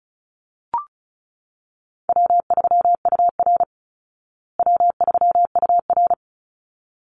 Morse code, heard as a keyed tone, about 34 wpm, 700 Hz. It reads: W3UR W3UR
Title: Easy Call Signs Repeated 2x - 35wpm